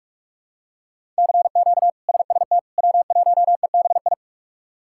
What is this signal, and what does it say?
Morse code, heard as a keyed tone, about 46 wpm, 700 Hz. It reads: CQ SST W1EBI
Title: K1USN SST QSO Practice - 45wpm